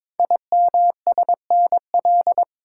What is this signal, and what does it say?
Morse code, heard as a keyed tone, about 22 wpm, 700 Hz. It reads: IMSNL
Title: Code Groups - 5 Letters - 22wpm